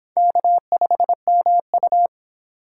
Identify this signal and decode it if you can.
Morse code, heard as a keyed tone, about 26 wpm, 700 Hz. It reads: K5MU